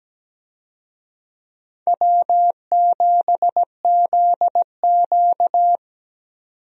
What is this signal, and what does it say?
Morse code, heard as a keyed tone, about 17 wpm, 700 Hz. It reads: W7ZQ
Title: USA Call Signs - 4 Characters - 17wpm